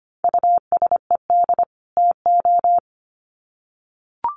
Morse code, heard as a keyed tone, about 25 wpm, 700 Hz. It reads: USED TO
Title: Sets of 2 Words 25wpm